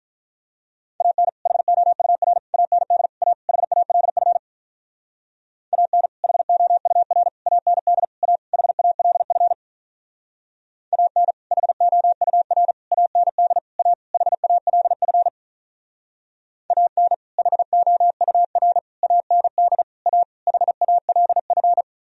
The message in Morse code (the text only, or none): AN HOUR AND A HALF AN HOUR AND A HALF AN HOUR AND A HALF AN HOUR AND A HALF